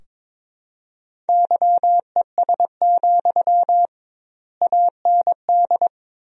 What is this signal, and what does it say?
Morse code, heard as a keyed tone, about 22 wpm, 700 Hz. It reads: YES, AND
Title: 2x Word-Spacing - Sentences from Top 500 Words - 22wpm